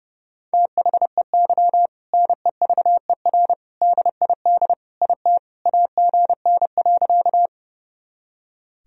Morse code, heard as a keyed tone, about 29 wpm, 700 Hz. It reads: THEY NEVER DID IT AGN.
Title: Sentences from Top 300 Words and Abbreviations 30wpm